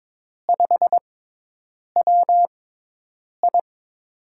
Morse code, heard as a keyed tone, about 22 wpm, spaced as for 8 wpm, 700 Hz. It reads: 5WI
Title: USA Call Signs - 5 Characters - 6x Character Spacing - 22wpm